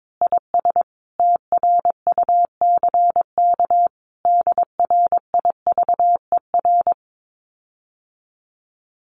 Morse code, heard as a keyed tone, about 22 wpm, 700 Hz. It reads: IS TRUCK DRIVER
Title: Common QSO Phrases Encore 22wpm